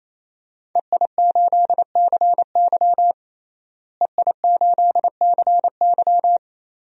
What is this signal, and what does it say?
Morse code, heard as a keyed tone, about 28 wpm, 700 Hz. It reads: EI8CY EI8CY
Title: DXCC-100 Call Signs - 5 Characters - Repeated 3x - 28wpm